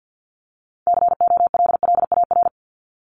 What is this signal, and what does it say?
Morse code, heard as a keyed tone, about 50 wpm, 700 Hz. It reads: COLLAR
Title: Top 1751-2000 Words 50wpm